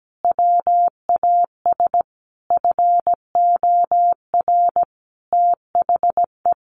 Morse code, heard as a keyed tone, about 17 wpm, 700 Hz. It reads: WAS FOR THE
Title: Sentences from Top 1400 Words 17wpm